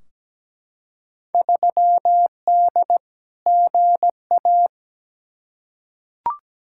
Morse code, heard as a keyed tone, about 17 wpm, 700 Hz. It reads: 3D GA
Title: Field Day Exchanges Encore - 17wpm